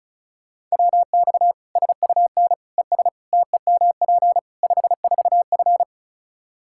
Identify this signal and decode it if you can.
Morse code, heard as a keyed tone, about 36 wpm, 700 Hz. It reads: WX SUN ES TEMP 54F
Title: Practice QSO - Exchange 2 - Info - 35wpm